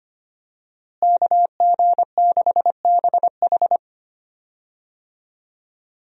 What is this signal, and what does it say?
Morse code, heard as a keyed tone, about 25 wpm, 700 Hz. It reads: KG6BH
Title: USA Call Signs - 5 Characters - Repeated 2x - 25wpm